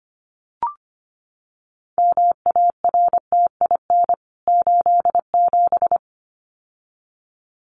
Morse code, heard as a keyed tone, about 25 wpm, 700 Hz. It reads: MARTIN 87
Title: CWT Exchange 25wpm